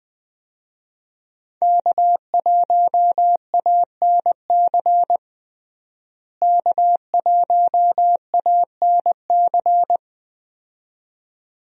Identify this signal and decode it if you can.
Morse code, heard as a keyed tone, about 20 wpm, 700 Hz. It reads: K1ANC K1ANC
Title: USA Call Signs - 5 Characters - Repeated 2x - 20wpm